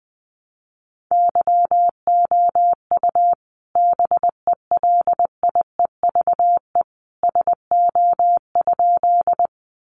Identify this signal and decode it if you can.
Morse code, heard as a keyed tone, about 20 wpm, 700 Hz. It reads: YOU BELIEVE SO?
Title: Sentences from Top 1000 Words 20wpm